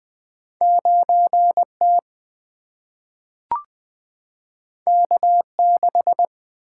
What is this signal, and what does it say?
Morse code, heard as a keyed tone, about 20 wpm, 700 Hz. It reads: 9 T K 6
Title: Single-Letter Single-Number 20wpm